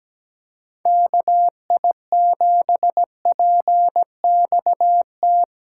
Morse code, heard as a keyed tone, about 17 wpm, 700 Hz. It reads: KI7PXT
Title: USA Call Signs - 6 Characters - 2x Character Spacing - 17wpm